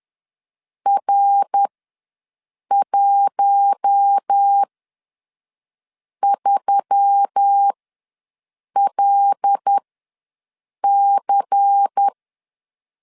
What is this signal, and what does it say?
Morse code, heard as a keyed tone, about 10 wpm, spaced as for 5 wpm, 800 Hz. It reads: R13LC
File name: LEZIONI / 5 LBVC / 5 LBVC ws 10wpm 3733